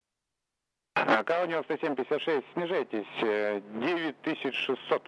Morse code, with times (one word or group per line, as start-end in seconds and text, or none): none